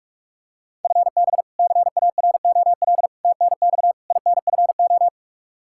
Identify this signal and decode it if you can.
Morse code, heard as a keyed tone, about 46 wpm, 700 Hz. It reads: FB CAROL TNX INFO
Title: Practice QSO - Exchange 3 - More Info - 45wpm